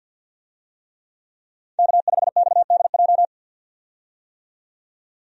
Morse code, heard as a keyed tone, about 52 wpm, 700 Hz. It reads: K5XDJ
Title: USA Call Signs - 5 Characters - 50wpm